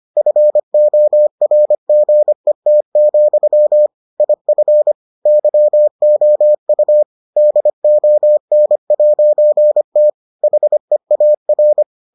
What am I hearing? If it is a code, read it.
Morse code, heard as a keyed tone, about 25 wpm, 600 Hz. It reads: FORGET, IF YOU DON'T HEAR